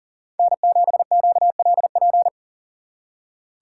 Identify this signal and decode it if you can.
Morse code, heard as a keyed tone, about 52 wpm, 700 Hz. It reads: N7QLP